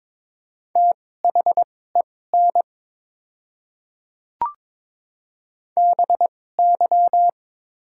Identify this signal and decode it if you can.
Morse code, heard as a keyed tone, about 22 wpm, spaced as for 16 wpm, 700 Hz. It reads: THEN BY
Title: 2x Character Spacing - Top 100 Words - 22wpm